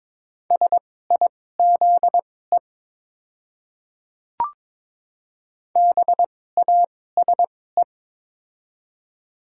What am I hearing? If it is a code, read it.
Morse code, heard as a keyed tone, about 22 wpm, spaced as for 16 wpm, 700 Hz. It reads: SIZE BASE
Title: 2x Character Spacing - Top 500 Words - 22wpm